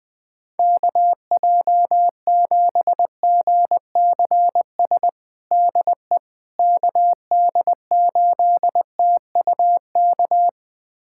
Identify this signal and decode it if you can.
Morse code, heard as a keyed tone, about 20 wpm, 700 Hz. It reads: KJ7GCS DE KD8TUK
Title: USA Callsign DE Callsign 20wpm